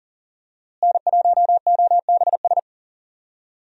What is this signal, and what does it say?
Morse code, heard as a keyed tone, about 40 wpm, 700 Hz. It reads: N1OBS
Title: USA Call Signs - 5 Characters - Repeated 3x - 40wpm